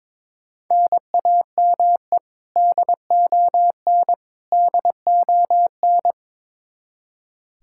Morse code, heard as a keyed tone, about 22 wpm, 700 Hz. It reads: NAME DON DON
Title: Practice QSO - Exchange 1 - Beginning - 22wpm